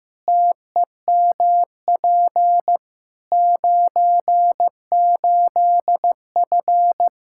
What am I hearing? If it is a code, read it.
Morse code, heard as a keyed tone, about 15 wpm, 700 Hz. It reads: TEMP 98F
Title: K1USN SST QSO Practice 15wpm